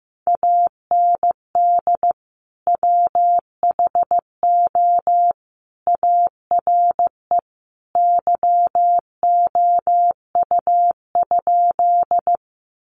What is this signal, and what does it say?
Morse code, heard as a keyed tone, about 15 wpm, 700 Hz. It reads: AND WHO ARE YOU?
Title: Sentences from Top 100 Words 15wpm